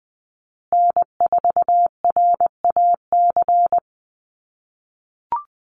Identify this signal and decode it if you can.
Morse code, heard as a keyed tone, about 20 wpm, 700 Hz. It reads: N4RAC
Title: USA Call Signs Repeated 2x 20wpm